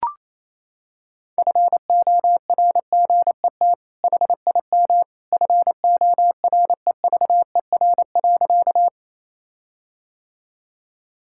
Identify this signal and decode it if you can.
Morse code, heard as a keyed tone, about 27 wpm, 700 Hz. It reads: FORGET HIM FOREVER.
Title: Sentences from Top 1600 Words 28wpm